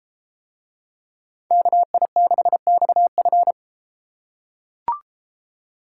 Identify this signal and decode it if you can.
Morse code, heard as a keyed tone, about 33 wpm, 700 Hz. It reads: KI6XF